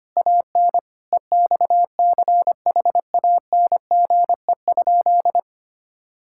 Morse code, heard as a keyed tone, about 25 wpm, 700 Hz. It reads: AN EXCHANGE?
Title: Sentences from Top 1000 Words - Review Part 6 - 25wpm